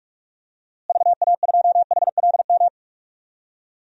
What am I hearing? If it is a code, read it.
Morse code, heard as a keyed tone, about 41 wpm, 700 Hz. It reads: VA2HLM
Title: DXCC-100 Call Signs - 6 Characters - Repeated 3x - 45wpm